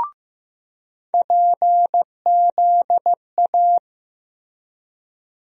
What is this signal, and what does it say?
Morse code, heard as a keyed tone, about 15 wpm, 700 Hz. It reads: PZA
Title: USA Call Sign Postfixes - 3 Letters 15wpm